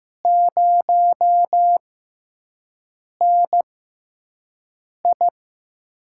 Morse code, heard as a keyed tone, about 15 wpm, spaced as for 5 wpm, 700 Hz. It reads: 0NI
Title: USA Call Signs - 4 Characters - 6x Character Spacing - 15wpm